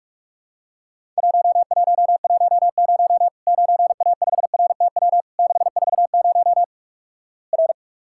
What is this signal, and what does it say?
Morse code, heard as a keyed tone, about 47 wpm, 700 Hz. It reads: 1110 9A5RTW 640 R
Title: Worked All Europe DX Contest - QTC Exchange - 45wpm